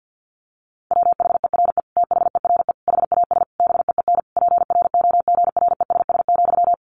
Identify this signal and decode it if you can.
Morse code, heard as a keyed tone, about 48 wpm, 700 Hz. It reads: WHERE THERE HAS BEEN PROGRESS,
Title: Sentences from Top 1400 Words 50wpm